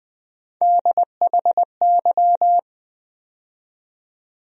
Morse code, heard as a keyed tone, about 21 wpm, 700 Hz. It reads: DHY
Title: USA Call Sign Postfixes - 3 Letters 20wpm